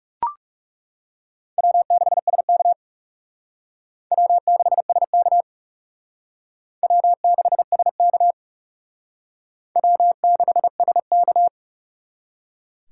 Morse code, heard as a keyed tone, about 37 wpm, 700 Hz. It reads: W6SK W6SK W6SK W6SK
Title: Speed Racing - USA 1x2 Call Signs 30wpm